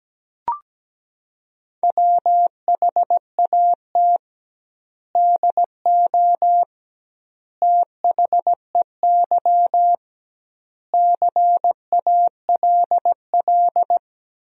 Morse code, heard as a keyed tone, about 17 wpm, 700 Hz. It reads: WHAT DO THEY CALL
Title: 2x Word-Spacing - Sentences from Top 100 Words - 17wpm